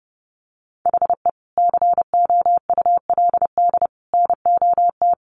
Morse code, heard as a keyed tone, about 30 wpm, 700 Hz. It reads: HE COULD NOT
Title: Sentences from Top 1500 Words 30wpm